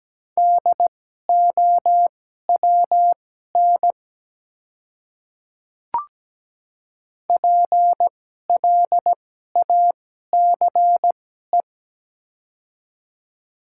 Morse code, heard as a keyed tone, about 17 wpm, spaced as for 12 wpm, 700 Hz. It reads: DOWN PLACE
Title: 2x Character Spacing - Top 200 Words - 17wpm